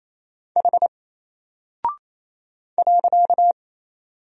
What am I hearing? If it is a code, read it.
Morse code, heard as a keyed tone, about 27 wpm, 700 Hz. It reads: H .
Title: Single Character (Letter:Number:Punctuation:Prosign) 28wpm